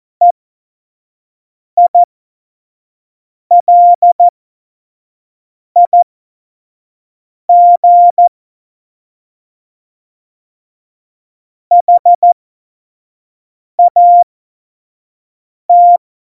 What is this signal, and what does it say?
Morse code, heard as a keyed tone, about 14 wpm, spaced as for 5 wpm, 700 Hz. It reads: EILIG HAT
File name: Morsekurs / Witze / Witz 14. Mai 2019, 25 F70